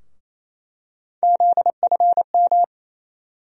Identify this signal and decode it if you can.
Morse code, heard as a keyed tone, about 27 wpm, 700 Hz. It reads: ZFM
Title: USA Call Sign Postfixes - 3 Letters 28wpm